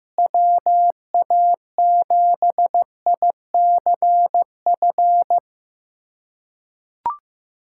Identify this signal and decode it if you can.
Morse code, heard as a keyed tone, about 16 wpm, 700 Hz. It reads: WA7ICF
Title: USA Callsign DE Callsign - 6 Characters - 15wpm